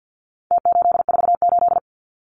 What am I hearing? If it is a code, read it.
Morse code, heard as a keyed tone, about 50 wpm, 700 Hz. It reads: T848
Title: Numbers - Contest Style - 4-Digit 50wpm